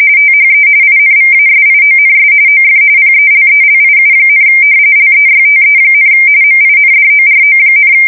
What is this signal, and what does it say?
RTTY, about 45.45 baud. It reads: THE INVASIONS OF PELELIU, LEYTE, MINDORO,
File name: fsk170 / battleshipnj